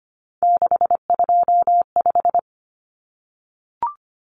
Morse code, heard as a keyed tone, about 25 wpm, 700 Hz. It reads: 625
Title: Numbers - 3-Digit 25wpm